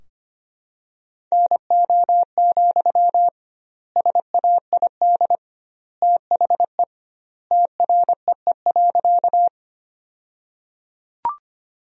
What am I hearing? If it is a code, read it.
Morse code, heard as a keyed tone, about 25 wpm, 700 Hz. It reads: NO, SAID THE TREE.
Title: 2x Word-Spacing - Sentences from Top 300 Words - 25wpm